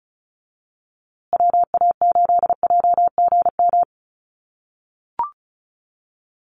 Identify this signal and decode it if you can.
Morse code, heard as a keyed tone, about 34 wpm, 700 Hz. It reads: WA8JGM
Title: USA Call Signs Repeated 2x 35wpm